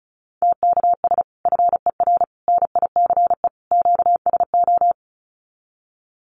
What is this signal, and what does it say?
Morse code, heard as a keyed tone, about 34 wpm, 700 Hz. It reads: TKS FER NICE QSO
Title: Common QSO Phrases 35wpm